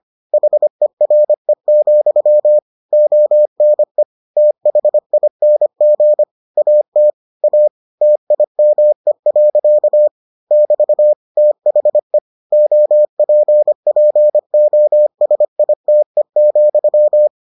Morse code, heard as a keyed tone, about 25 wpm, 600 Hz. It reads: HERE, ONE THING AT A TIME. <BT> THE OPPOSITE,